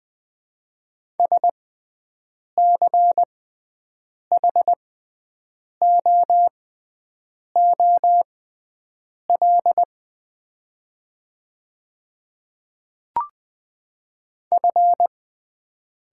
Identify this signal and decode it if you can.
Morse code, heard as a keyed tone, about 20 wpm, spaced as for 7 wpm, 690 Hz. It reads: SCHOOL F
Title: 6x Character Spacing - Top 300 Words - 20wpm